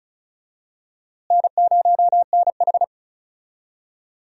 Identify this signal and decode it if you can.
Morse code, heard as a keyed tone, about 36 wpm, 700 Hz. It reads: N0NH
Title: USA Call Signs - 4 Characters - Repeated 2x - 35wpm